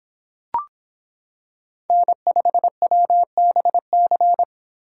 Morse code, heard as a keyed tone, about 25 wpm, 700 Hz. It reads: N5WBC